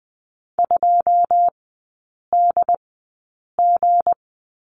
Morse code, heard as a keyed tone, about 20 wpm, spaced as for 8 wpm, 700 Hz. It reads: 2DG
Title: 159 Lesson 26e - Letter g - Three Characters 20wpm